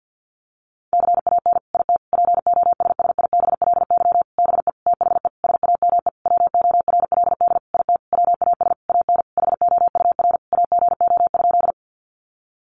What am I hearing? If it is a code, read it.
Morse code, heard as a keyed tone, about 49 wpm, 700 Hz. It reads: CAN IT POSSIBLY BE THE SAME WORLD IT WAS AN HOUR AGO?
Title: Sentences from Top 1300 Words 50wpm